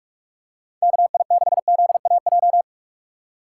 Morse code, heard as a keyed tone, about 44 wpm, 700 Hz. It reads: KI6ZAJ
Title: USA Call Signs - 6 Characters - 6x Character Spacing - 45wpm